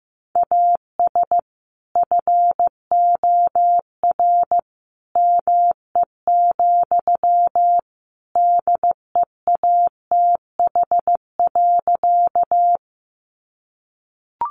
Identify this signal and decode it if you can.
Morse code, heard as a keyed tone, about 15 wpm, 700 Hz. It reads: AS FOR ME, DEATH.
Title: Sentences from Top 1100 Words 15wpm